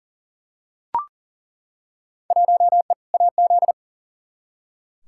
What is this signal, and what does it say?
Morse code, heard as a keyed tone, about 40 wpm, 700 Hz. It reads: 1E AZ
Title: Field Day Exchanges - 40wpm